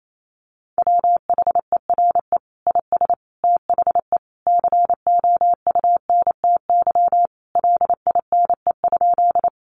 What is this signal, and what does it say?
Morse code, heard as a keyed tone, about 28 wpm, 700 Hz. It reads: WHERE IS THE COUNTY LINE?
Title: Sentences from Top 1200 Words 28wpm